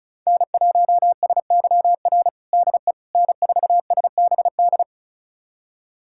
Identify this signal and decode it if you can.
Morse code, heard as a keyed tone, about 36 wpm, 700 Hz. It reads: N1SYR DE N4SBD
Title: USA Callsign DE Callsign - 5 Characters - 35wpm